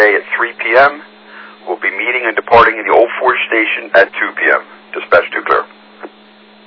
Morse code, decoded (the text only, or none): none